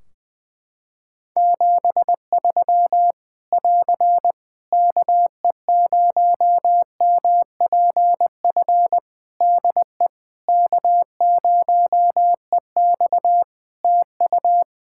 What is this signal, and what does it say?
Morse code, heard as a keyed tone, about 20 wpm, 700 Hz. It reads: 73 <AR> KE0MPF DE K0EX TU
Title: Practice QSO - Exchange 4 - Ending - 20wpm